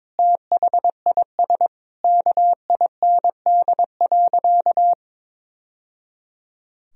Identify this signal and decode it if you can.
Morse code, heard as a keyed tone, about 22 wpm, 700 Hz. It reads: THIS KIND.
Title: Sentences from Top 1000 Words - Review Part 4 - 22wpm